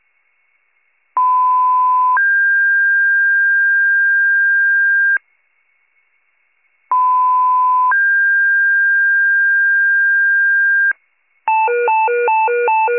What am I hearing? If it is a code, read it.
Two-tone page, heard as a one-second tone, then a three-second tone, about 1010 then 1600 Hz, about 1010 then 1600 Hz. It is heard from 1.2 to 10.9 s.